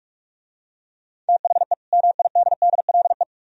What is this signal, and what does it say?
Morse code, heard as a keyed tone, about 46 wpm, 700 Hz. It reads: THE MIDDLE